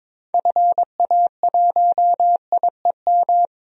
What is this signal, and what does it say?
Morse code, heard as a keyed tone, about 22 wpm, 700 Hz. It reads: FA1IEM
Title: DXCC-100 Call Signs - 6 Characters - 22wpm